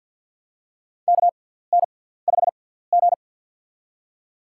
Morse code, heard as a keyed tone, about 53 wpm, spaced as for 17 wpm, 700 Hz. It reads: KN5G
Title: USA Call Signs - 4 Characters - 6x Character Spacing - 50wpm